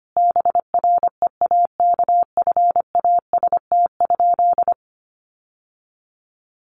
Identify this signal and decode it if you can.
Morse code, heard as a keyed tone, about 25 wpm, 700 Hz. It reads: BREAKFAST?
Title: Sentences from Top 1500 Words 25wpm